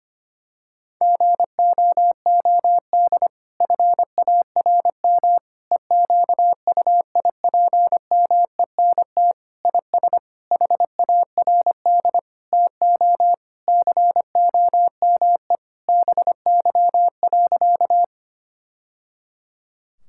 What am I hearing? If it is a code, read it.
Morse code, heard as a keyed tone, about 25 wpm, 700 Hz. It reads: GOOD FARM EQUIPMENT IS HARD TO COME BY.
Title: Sentences from Top 1000 Words - Review Part 5 - 25wpm